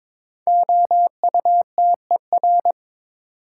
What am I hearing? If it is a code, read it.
Morse code, heard as a keyed tone, about 22 wpm, 700 Hz. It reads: OUTER
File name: practice / 22wpm / 5 Letter Words - First 3 Letters Given - 22wpm